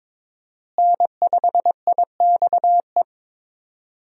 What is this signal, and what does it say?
Morse code, heard as a keyed tone, about 22 wpm, 700 Hz. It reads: N5IXE
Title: USA Call Signs - 5 Characters - Repeated 3x - 22wpm